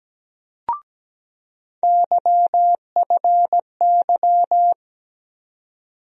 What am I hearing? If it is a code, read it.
Morse code, heard as a keyed tone, about 17 wpm, 700 Hz. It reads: YFY
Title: USA Call Sign Postfixes - 3 Letters 17wpm